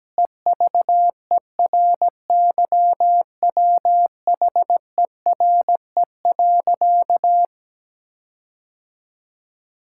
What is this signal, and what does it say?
Morse code, heard as a keyed tone, about 17 wpm, 700 Hz. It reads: EVERYWHERE.
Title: Sentences from Top 1600 Words 17wpm